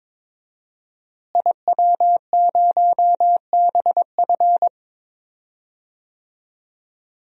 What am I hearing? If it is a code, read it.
Morse code, heard as a keyed tone, about 22 wpm, 700 Hz. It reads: IW0BF
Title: DXCC-100 Call Signs - 5 Characters - Repeated 2x - 22wpm